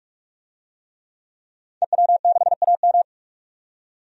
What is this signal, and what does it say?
Morse code, heard as a keyed tone, about 46 wpm, 700 Hz. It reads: EW6AM
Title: DXCC-100 Call Signs - 5 Characters - 45wpm